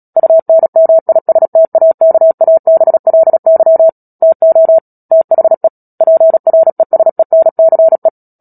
Morse code, heard as a keyed tone, about 37 wpm, 650 Hz. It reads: UNMISTAKABLY TO THE PRESENCE
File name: Book Two / 10-Chapter 10 - Epilogue - 36 WPM